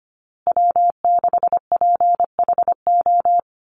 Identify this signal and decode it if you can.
Morse code, heard as a keyed tone, about 25 wpm, 700 Hz. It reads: W6PHO
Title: Call Signs Repeated 2x - 25wpm